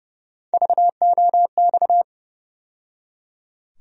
Morse code, heard as a keyed tone, about 30 wpm, 700 Hz. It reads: VOX
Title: USA Call Sign Postfixes - 3 Letters 30wpm